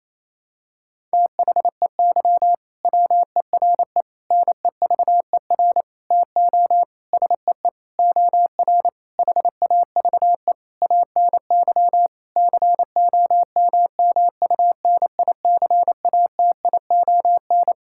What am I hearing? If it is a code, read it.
Morse code, heard as a keyed tone, about 28 wpm, 700 Hz. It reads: THEY WERE NEVER TO SEE OR HAVE ANY COMMUNICATION
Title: Sentences from Top 1000 Words - Review Part 6 - 28wpm